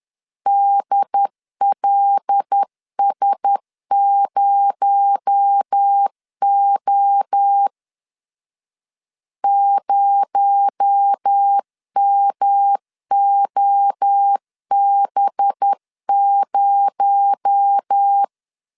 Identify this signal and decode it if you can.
Morse code, heard as a keyed tone, about 10 wpm, 800 Hz. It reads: DLS0O 0MOB0
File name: LEZIONI / 6 PKX0 / 6 PKX0 10wpm 2707